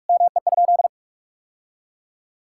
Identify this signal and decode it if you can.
Morse code, heard as a keyed tone, about 46 wpm, 700 Hz. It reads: ME?
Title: Sentences from Top 1600 Words 45wpm